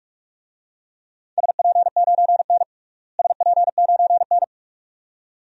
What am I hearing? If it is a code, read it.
Morse code, heard as a keyed tone, about 46 wpm, 700 Hz. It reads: SP9N SP9N
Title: DXCC-100 Call Signs - 4 Characters - Repeated 2x - 45wpm